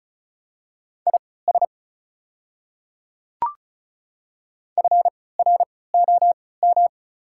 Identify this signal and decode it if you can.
Morse code, heard as a keyed tone, about 36 wpm, spaced as for 20 wpm, 700 Hz. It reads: IS FROM
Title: 3x Character Spacing - Top 100 Words - 35wpm